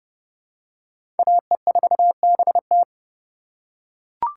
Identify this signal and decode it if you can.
Morse code, heard as a keyed tone, about 30 wpm, 700 Hz. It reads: AE4BT
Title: USA Call Signs - 5 Characters - 8x Character Spacing - 30wpm